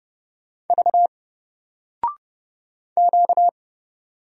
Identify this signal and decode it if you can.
Morse code, heard as a keyed tone, about 29 wpm, 700 Hz. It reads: V Q
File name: Single-Letter-Number-30wpm